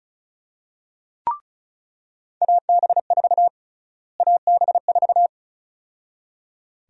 Morse code, heard as a keyed tone, about 34 wpm, 700 Hz. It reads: AB4 AB4
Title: USA Call Sign Prefixes - Repeated 2x - 35wpm